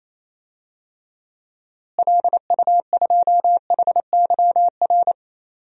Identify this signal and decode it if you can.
Morse code, heard as a keyed tone, about 28 wpm, 700 Hz. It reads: LU2HYR
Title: DXCC-100 Call Signs - 6 Characters - Repeated 3x - 28wpm